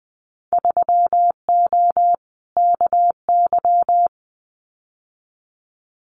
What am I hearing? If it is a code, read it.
Morse code, heard as a keyed tone, about 20 wpm, 700 Hz. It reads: 3O KY
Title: Winter Field Day Exchanges 20wpm